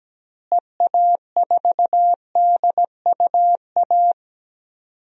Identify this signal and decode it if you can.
Morse code, heard as a keyed tone, about 17 wpm, 700 Hz. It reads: EA4DUA
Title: DXCC-100 Call Signs - 6 Characters - Repeated 3x - 17wpm